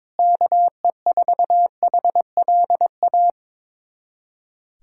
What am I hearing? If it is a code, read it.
Morse code, heard as a keyed tone, about 22 wpm, 700 Hz. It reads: KE4HLA
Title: USA Callsign DE Callsign - 6 Characters - 22wpm